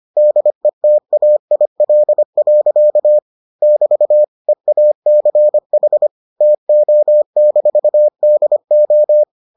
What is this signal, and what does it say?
Morse code, heard as a keyed tone, about 25 wpm, 600 Hz. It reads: DETAIL. <BT> EACH TO-DO